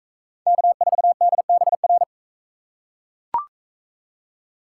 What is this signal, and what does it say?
Morse code, heard as a keyed tone, about 41 wpm, 700 Hz. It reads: K4DBR